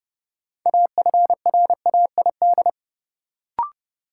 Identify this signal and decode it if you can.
Morse code, heard as a keyed tone, about 32 wpm, 700 Hz. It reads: AFRAID
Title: Sentences from Top 1500 Words - New Word Drill Alt - 30wpm